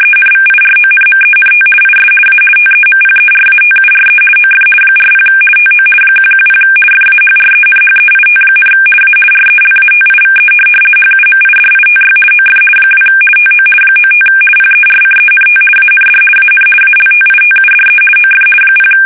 RTTY, about 50 baud. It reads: ENVIRONMENT IS A TEMPERATURE OF 40 TO 110 FAHRENHEIT, A RELATIVE HUMIDITY OF 2 TO 95 PERCENT AND AN ALTITUDE OF 0
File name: fsk850-50B / teletype-corp